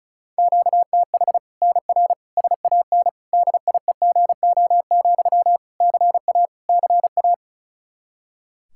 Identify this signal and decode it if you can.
Morse code, heard as a keyed tone, about 36 wpm, 700 Hz. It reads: QTH NR SAN DIEGO, CA CA
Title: Practice QSO - Exchange 1 - Beginning - 35wpm